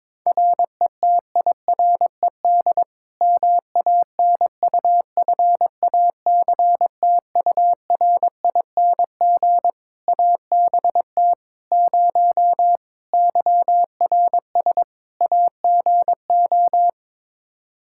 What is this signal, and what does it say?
Morse code, heard as a keyed tone, about 22 wpm, 700 Hz. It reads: RETIRED MANUFACTURING ABT 0 YRS AGO
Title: Practice QSO - Exchange 3 - More Info - 22wpm